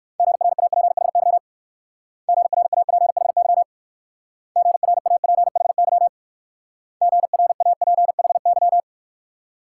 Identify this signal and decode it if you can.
Morse code, heard as a keyed tone, about 57 wpm, 690 Hz. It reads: GRAPHY GRAPHY GRAPHY GRAPHY